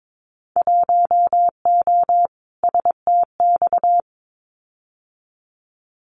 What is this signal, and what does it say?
Morse code, heard as a keyed tone, about 22 wpm, 700 Hz. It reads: 1O STX
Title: Winter Field Day Exchanges 22wpm